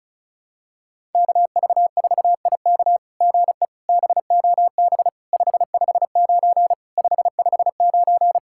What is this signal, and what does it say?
Morse code, heard as a keyed tone, about 36 wpm, 700 Hz. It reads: KV4IK GE BOB 559 559
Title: SOTA QSO Practice 35wpm